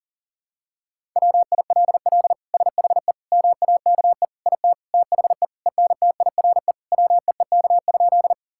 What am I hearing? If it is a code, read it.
Morse code, heard as a keyed tone, about 40 wpm, 700 Hz. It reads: WILL SHE MAKE IT THE ENTIRE WEEK?
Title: Sentences from Top 1000 Words - Review Part 5 - 40wpm